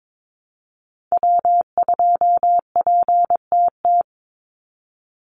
Morse code, heard as a keyed tone, about 22 wpm, 700 Hz. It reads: W2PTT
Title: Call Signs - 22wpm